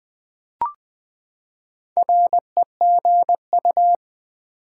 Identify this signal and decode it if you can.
Morse code, heard as a keyed tone, about 21 wpm, 700 Hz. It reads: REGU
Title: Build A Word - 4 Letters At A Time - 10-Letter Words - 20wpm